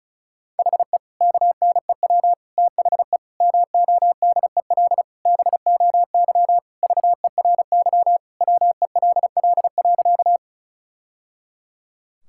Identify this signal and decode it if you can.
Morse code, heard as a keyed tone, about 34 wpm, 700 Hz. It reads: HE KNEW THE MODEL BOY VERY WELL.
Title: Sentences from Top 1000 Words - Review Part 5 - 35wpm